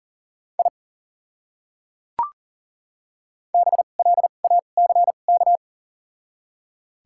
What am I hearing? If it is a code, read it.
Morse code, heard as a keyed tone, about 40 wpm, spaced as for 29 wpm, 700 Hz. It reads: I BLACK
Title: 2x Character Spacing - Top 500 Words - 40wpm